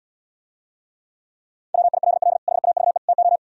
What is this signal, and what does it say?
Morse code, heard as a keyed tone, about 77 wpm, 700 Hz. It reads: WILL HAVE TO